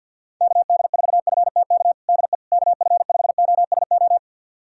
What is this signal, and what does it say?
Morse code, heard as a keyed tone, about 46 wpm, 700 Hz. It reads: KD4FTK DE KF5OSO
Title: USA Callsign DE Callsign 50wpm